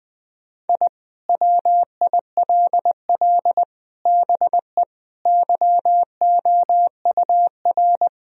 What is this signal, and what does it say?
Morse code, heard as a keyed tone, about 20 wpm, 700 Hz. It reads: I WILL BE YOUR
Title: Sentences from Top 1000 Words - Review Part 4 - 20wpm